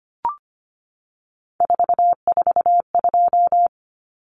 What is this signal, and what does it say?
Morse code, heard as a keyed tone, about 25 wpm, 700 Hz. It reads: 442
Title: Numbers - 3-Digit 25wpm